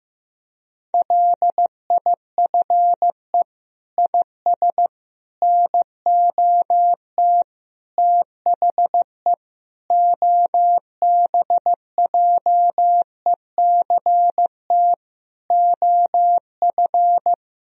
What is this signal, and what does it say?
Morse code, heard as a keyed tone, about 15 wpm, 700 Hz. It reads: LIFE IS NOT THE OBJECT OF
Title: Sentences from Top 1000 Words - Review Part 3 - 15wpm